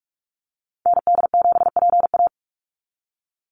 Morse code, heard as a keyed tone, about 44 wpm, 700 Hz. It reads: ND7PA
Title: NA SOTA Activator Callsigns 45wpm